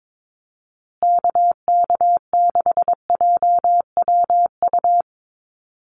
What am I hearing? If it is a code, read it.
Morse code, heard as a keyed tone, about 22 wpm, 700 Hz. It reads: KK6JWU
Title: USA Call Signs Repeated 3x 22wpm